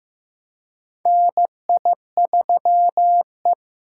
Morse code, heard as a keyed tone, about 15 wpm, 700 Hz. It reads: NI3E
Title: USA Call Signs - 4 Characters - 2x Character Spacing - 15wpm